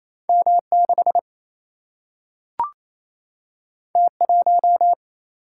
Morse code, heard as a keyed tone, about 28 wpm, 700 Hz. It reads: M 6 T 1
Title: Single-Letter Single-Number 28wpm